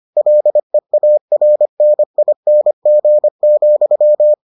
Morse code, heard as a keyed tone, about 25 wpm, 600 Hz. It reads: LEARNING,